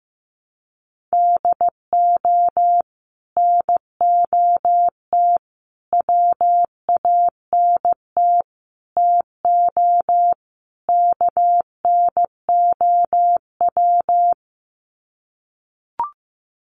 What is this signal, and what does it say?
Morse code, heard as a keyed tone, about 15 wpm, 700 Hz. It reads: DO NOT WANT TO KNOW
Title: Sets of 5 Words Encore 15wpm